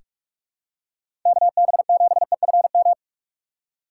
Morse code, heard as a keyed tone, about 46 wpm, 700 Hz. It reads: KB7EFM
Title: USA Call Signs - 6x Character Spacing - 45wpm